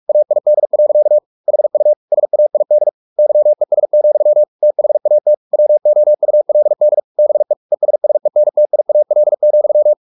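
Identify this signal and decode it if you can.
Morse code, heard as a keyed tone, about 44 wpm, 600 Hz. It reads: AID. HU SAID YES, THAT WOULD BE ESSENTIAL,
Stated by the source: Morse practice at 45 wpm, the novel Never by Ken Follett in English, from CW med Gnister.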